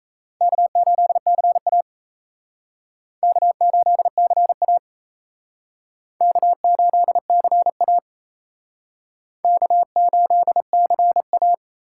Morse code, heard as a keyed tone, about 35 wpm, 700 Hz. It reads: K8CA K8CA K8CA K8CA